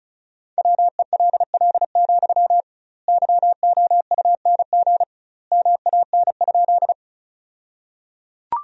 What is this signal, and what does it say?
Morse code, heard as a keyed tone, about 34 wpm, 700 Hz. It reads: WELL, YOUNG MAN?
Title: 2x Word-Spacing - Sentences from Top 400 Words - 35wpm